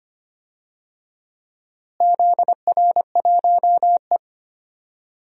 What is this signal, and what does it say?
Morse code, heard as a keyed tone, about 25 wpm, 700 Hz. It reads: ZR1E
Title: DXCC-100 Call Signs - 4 Characters - Repeated 2x - 25wpm